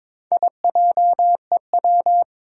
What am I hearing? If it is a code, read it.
Morse code, heard as a keyed tone, about 22 wpm, 700 Hz. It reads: IJEW
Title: Four Challenging Letter-Number Combinations 22wpm